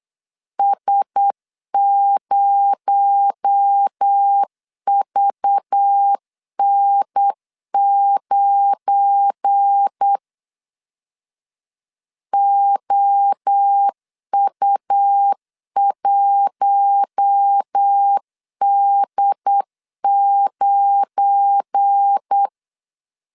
Morse code, heard as a keyed tone, about 8 wpm, 800 Hz. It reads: S0VN9 OU1D9
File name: LEZIONI / 7 GF9Z / 7 GF9 8wpm 3545